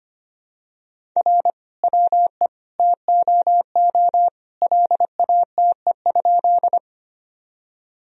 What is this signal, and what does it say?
Morse code, heard as a keyed tone, about 25 wpm, 700 Hz. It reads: R WE TOO LATE?
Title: Sentences from Top 300 Words and Abbreviations 25wpm